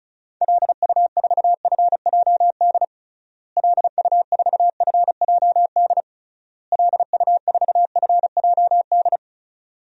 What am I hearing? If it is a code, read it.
Morse code, heard as a keyed tone, about 36 wpm, 700 Hz. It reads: LU4FJD LU4FJD LU4FJD
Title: DXCC-100 Call Signs - 6 Characters - Repeated 3x - 35wpm